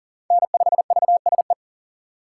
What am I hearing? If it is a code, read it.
Morse code, heard as a keyed tone, about 40 wpm, 700 Hz. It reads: N5VSE
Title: USA Call Signs - 5 Characters - 2x Character Spacing - 40wpm